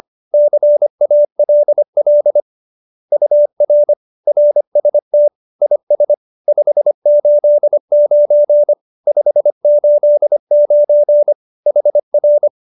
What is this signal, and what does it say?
Morse code, heard as a keyed tone, about 25 wpm, 590 Hz. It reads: CALL UR RST IS 589 589 HR